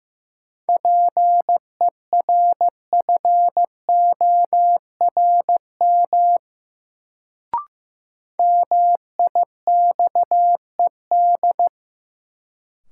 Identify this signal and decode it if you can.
Morse code, heard as a keyed tone, about 15 wpm, 700 Hz. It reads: PERFORM MIXED
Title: Sentences from Top 1500 Words - New Word Drill Alt - 15wpm